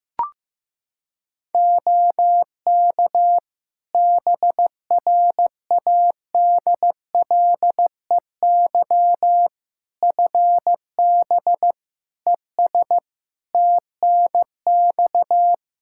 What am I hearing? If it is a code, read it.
Morse code, heard as a keyed tone, about 15 wpm, 700 Hz. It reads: OK BRADLEY FB ES TNX
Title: Practice QSO - Exchange 2 - Info - 15wpm